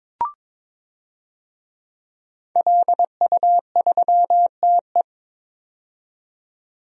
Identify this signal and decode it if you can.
Morse code, heard as a keyed tone, about 22 wpm, 700 Hz. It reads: LU3TE
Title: DXCC-100 Call Signs - 5 Characters - 22wpm